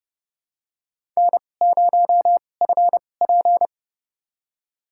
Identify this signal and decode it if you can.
Morse code, heard as a keyed tone, about 30 wpm, spaced as for 22 wpm, 700 Hz. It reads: N0FP
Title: USA Call Signs - 4 Characters - 2x Character Spacing - 30wpm